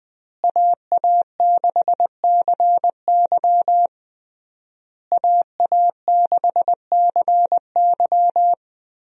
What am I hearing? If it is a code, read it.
Morse code, heard as a keyed tone, about 20 wpm, 700 Hz. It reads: AA6CY AA6CY
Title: USA Call Signs - 5 Characters - Repeated 3x - 20wpm